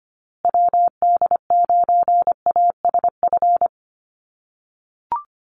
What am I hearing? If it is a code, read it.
Morse code, heard as a keyed tone, about 25 wpm, 700 Hz. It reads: WD9ASF
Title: USA Call Signs Repeated 2x 25wpm